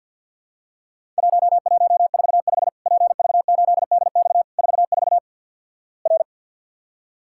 Morse code, heard as a keyed tone, about 50 wpm, 700 Hz. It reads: 1145 PV8DX 44 R
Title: Worked All Europe DX Contest - QTC Exchange with Repeats - 50wpm